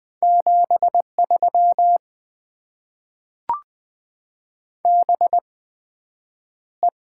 Morse code, heard as a keyed tone, about 20 wpm, spaced as for 9 wpm, 700 Hz. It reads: 73 BE
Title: Top 90 QSO Elements - 8x Character Spacing - 20wpm